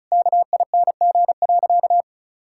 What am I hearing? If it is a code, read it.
Morse code, heard as a keyed tone, about 34 wpm, 700 Hz. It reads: KING.
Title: Sentences from Top 1000 Words - Review Part 6 - 35wpm